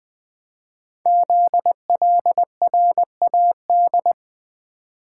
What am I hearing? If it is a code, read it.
Morse code, heard as a keyed tone, about 20 wpm, 700 Hz. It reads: ZLRAD
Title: Code Groups - 5 Letter:Numbers - 20wpm